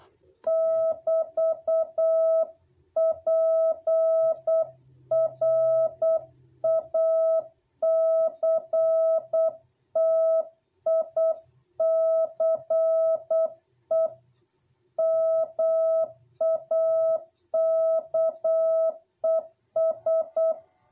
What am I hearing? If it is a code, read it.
Morse code, heard as a keyed tone, about 8 wpm, 660 Hz. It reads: <BT>PRACTICE MAKES
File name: jh8cbh / 2021cw